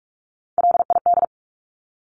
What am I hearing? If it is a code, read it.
Morse code, heard as a keyed tone, about 44 wpm, 700 Hz. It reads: LID
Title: Abbreviations Repeated 3x - 45wpm